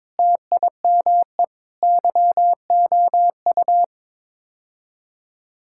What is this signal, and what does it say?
Morse code, heard as a keyed tone, about 22 wpm, 700 Hz. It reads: TIME YOU
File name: practice / 22wpm / Sets of 6 Words - 22wpm